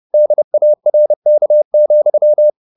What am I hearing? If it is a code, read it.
Morse code, heard as a keyed tone, about 30 wpm, 600 Hz. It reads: DARK,
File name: the old man and the sea / the old man and the sea 1 30wpm